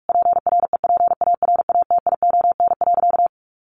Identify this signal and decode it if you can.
Morse code, heard as a keyed tone, about 44 wpm, 700 Hz. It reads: PREPARATION.
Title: Sentences from Top 1400 Words 45wpm